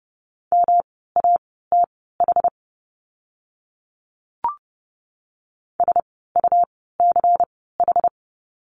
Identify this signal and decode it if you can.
Morse code, heard as a keyed tone, about 30 wpm, spaced as for 17 wpm, 700 Hz. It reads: MATH SUCH
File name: course / 30:10wpm / 108 Lesson 18f - Letter m - Words 30wpm